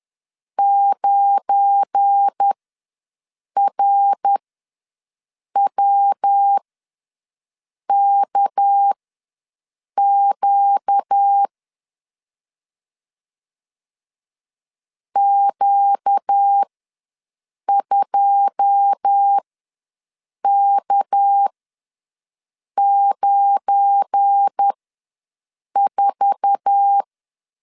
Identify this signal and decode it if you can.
Morse code, heard as a keyed tone, about 10 wpm, spaced as for 6 wpm, 800 Hz. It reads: 9RWKQ Q2K94
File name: LEZIONI / 9 2 ws 10wpm 4130